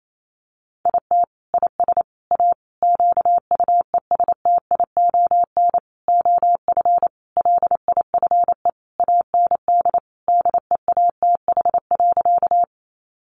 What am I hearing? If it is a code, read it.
Morse code, heard as a keyed tone, about 28 wpm, 700 Hz. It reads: IT IS A QUESTION OF LIFE AND DEATH.
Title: Sentences from Top 1100 Words 28wpm